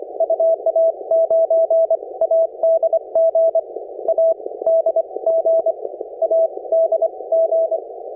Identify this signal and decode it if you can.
Morse code, heard as a keyed tone, about 24 wpm, spaced as for 20 wpm, 660 Hz. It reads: UA9ADG ADG ADG
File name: attachments / 7011cwPileup